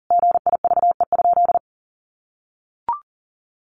Morse code, heard as a keyed tone, about 37 wpm, 700 Hz. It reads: GIVE?
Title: Sentences from Top 200 Words 40wpm